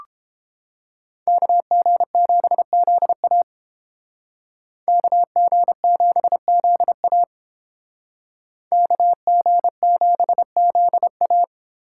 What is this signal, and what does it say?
Morse code, heard as a keyed tone, about 29 wpm, 700 Hz. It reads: KG7ZA KG7ZA KG7ZA